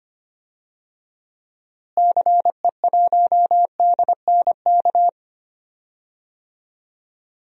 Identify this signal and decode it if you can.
Morse code, heard as a keyed tone, about 25 wpm, 700 Hz. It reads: CE1DNK
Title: DXCC-100 Call Signs - 6 Characters - Repeated 2x - 25wpm